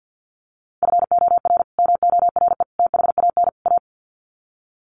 Morse code, heard as a keyed tone, about 50 wpm, 700 Hz. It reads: FOR MORE THAN A
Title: Sets of 4 Words 50wpm